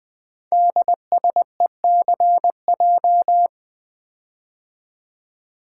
Morse code, heard as a keyed tone, about 20 wpm, 700 Hz. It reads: DSECJ
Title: Code Groups - 5 Letters - 20wpm